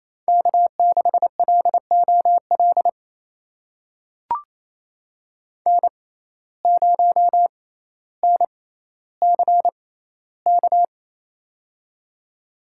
Morse code, heard as a keyed tone, about 28 wpm, spaced as for 14 wpm, 700 Hz. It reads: K6LOL N0NCK
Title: USA Call Signs - 5 Characters - 6x Character Spacing - 28wpm